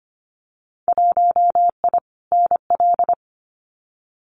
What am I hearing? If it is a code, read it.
Morse code, heard as a keyed tone, about 25 wpm, 700 Hz. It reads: 1I NL
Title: Winter Field Day Exchanges 25wpm